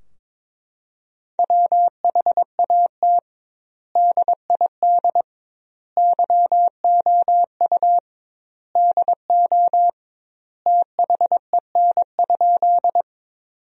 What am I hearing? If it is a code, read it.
Morse code, heard as a keyed tone, about 22 wpm, 700 Hz. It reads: WHAT DID YOU DO THEN?
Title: 2x Word-Spacing - Sentences from Top 100 Words - 22wpm